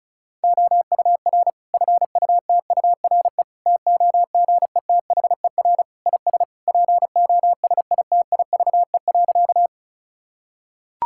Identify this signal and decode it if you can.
Morse code, heard as a keyed tone, about 36 wpm, 700 Hz. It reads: OUR FUTURE TOGETHER IS POSITIVE.
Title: Sentences from Top 1000 Words - Review Part 5 - 35wpm